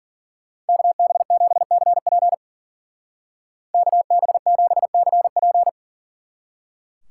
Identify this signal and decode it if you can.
Morse code, heard as a keyed tone, about 49 wpm, 700 Hz. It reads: KB7CP KB7CP
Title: Speed Racing - USA 2x2 Call Signs 40wpm